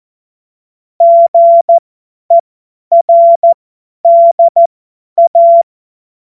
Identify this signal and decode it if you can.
Morse code, heard as a keyed tone, about 14 wpm, spaced as for 10 wpm, 680 Hz. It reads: GERDA
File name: Morsekurs / Witze / Witz 01. November 2022 50 F70